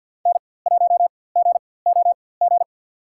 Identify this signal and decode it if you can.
Morse code, heard as a keyed tone, about 52 wpm, spaced as for 23 wpm, 700 Hz. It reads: N1GOG
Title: USA Call Signs - 5 Characters - 4x Character Spacing - 50wpm